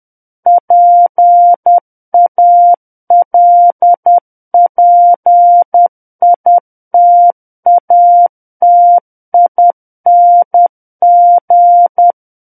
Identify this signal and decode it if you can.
Morse code, heard as a keyed tone, about 10 wpm, 700 Hz. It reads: PALPITATING
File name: Book One / 01-Chapter 1 - 10 WPM